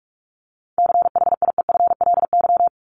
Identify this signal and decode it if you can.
Morse code, heard as a keyed tone, about 44 wpm, 700 Hz. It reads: CHIEFLY
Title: LY Suffix - Top 100 Words with Base Words 45wpm